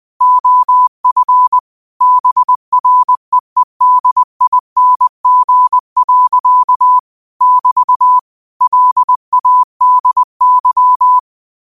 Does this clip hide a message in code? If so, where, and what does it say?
Yes: Morse code from 0.2 to 11.2 s, heard as a keyed tone, about 20 wpm, 1000 Hz. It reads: OF BREEDING. <BT> LADY